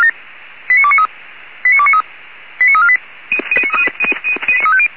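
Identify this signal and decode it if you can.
Five-tone selective call (ZVEI), heard as quick runs of five tones, tones of about 70 ms. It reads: 89293 89293 89348
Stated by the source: ZVEI five-tone selective call 89248 received by SDR.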